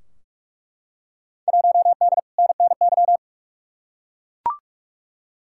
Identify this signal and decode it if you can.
Morse code, heard as a keyed tone, about 47 wpm, 700 Hz. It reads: 1D NNY
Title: Field Day Exchanges - 45wpm